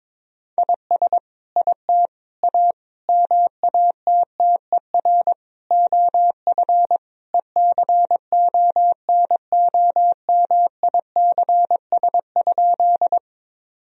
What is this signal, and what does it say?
Morse code, heard as a keyed tone, about 22 wpm, 700 Hz. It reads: IS IT A MATTER OF ECONOMICS?
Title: Sentences from Top 1000 Words - Review Part 3 - 22wpm